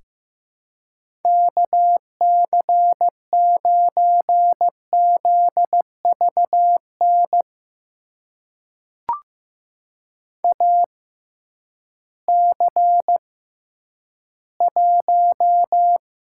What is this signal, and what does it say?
Morse code, heard as a keyed tone, about 15 wpm, spaced as for 10 wpm, 700 Hz. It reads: KC9ZVN AC1
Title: USA Call Signs - 6x Character Spacing - 15wpm